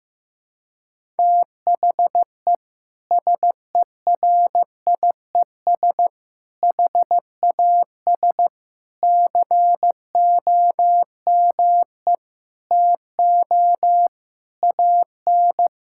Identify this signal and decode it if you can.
Morse code, heard as a keyed tone, about 15 wpm, 700 Hz. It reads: THE SERIES HAS COME TO AN
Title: Sentences from Top 1000 Words - Review Part 5 - 15wpm